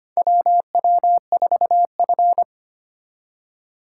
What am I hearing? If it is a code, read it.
Morse code, heard as a keyed tone, about 25 wpm, 700 Hz. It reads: WW4F
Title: USA Call Signs - 4 Characters - 2x Character Spacing - 25wpm